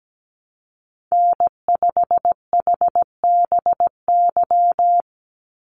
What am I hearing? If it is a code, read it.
Morse code, heard as a keyed tone, about 17 wpm, 700 Hz. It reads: N5HBY
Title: USA Call Signs 17wpm